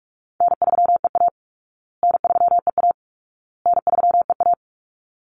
Morse code, heard as a keyed tone, about 44 wpm, 700 Hz. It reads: N3EA N3EA N3EA
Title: Easy Call Signs Repeated 3x - 45wpm